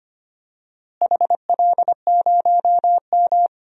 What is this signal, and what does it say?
Morse code, heard as a keyed tone, about 25 wpm, 700 Hz. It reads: HL0M
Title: DXCC-100 Call Signs - 4 Characters - Repeated 3x - 25wpm